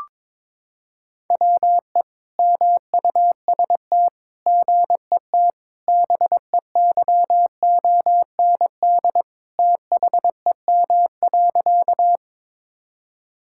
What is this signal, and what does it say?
Morse code, heard as a keyed tone, about 22 wpm, 700 Hz. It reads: WE MUST GET BEYOND THEM.
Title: Sentences from Top 1000 Words - Review Part 6 - 22wpm